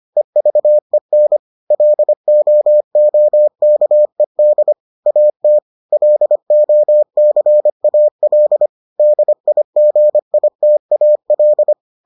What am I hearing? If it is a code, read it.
Morse code, heard as a keyed tone, about 25 wpm, 600 Hz. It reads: EVEN LOOKED AT LOCAL DIGITAL